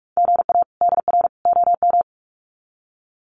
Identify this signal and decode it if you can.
Morse code, heard as a keyed tone, about 44 wpm, 700 Hz. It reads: GA DR OM
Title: Common QSO Phrases 45wpm